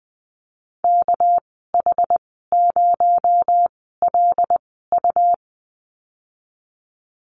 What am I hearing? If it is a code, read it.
Morse code, heard as a keyed tone, about 19 wpm, spaced as for 14 wpm, 700 Hz. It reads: KH0LU
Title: 219 Lesson 35g - Number 0 - Call Signs 20wpm